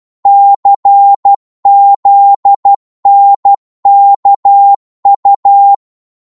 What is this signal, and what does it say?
Morse code, heard as a keyed tone, about 12 wpm, 800 Hz. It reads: CZNKU
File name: Morse / ltr-12wpm